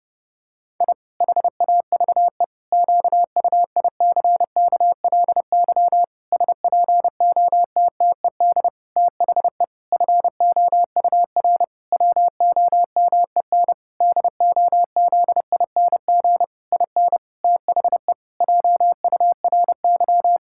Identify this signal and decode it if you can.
Morse code, heard as a keyed tone, about 30 wpm, 700 Hz. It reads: I HAVE QUICKLY SPOTTED THE FOUR WOMEN DOZING IN THE JURY
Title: Pangram Challenge v2 - 30wpm